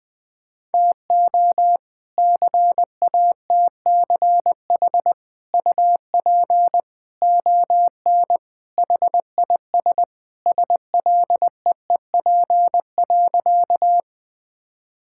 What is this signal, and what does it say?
Morse code, heard as a keyed tone, about 20 wpm, 700 Hz. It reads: TO CATCH UP ON HIS SLEEP.
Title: All-Age Appropriate Jokes - 20wpm